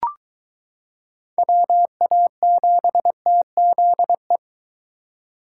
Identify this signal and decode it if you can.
Morse code, heard as a keyed tone, about 22 wpm, 700 Hz. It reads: WA7TZE